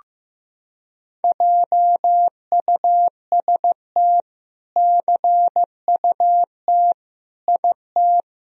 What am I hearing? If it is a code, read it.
Morse code, heard as a keyed tone, about 16 wpm, 700 Hz. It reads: JUST CUT IT
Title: Sentences from Top 300 Words and Abbreviations 15wpm